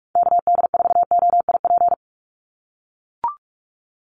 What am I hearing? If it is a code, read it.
Morse code, heard as a keyed tone, about 44 wpm, 700 Hz. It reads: KD4OIP
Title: USA Call Signs 45wpm